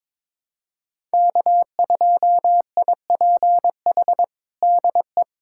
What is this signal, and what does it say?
Morse code, heard as a keyed tone, about 21 wpm, 700 Hz. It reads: K2IPH DE
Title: USA Callsign DE Callsign - 5 Characters - 22wpm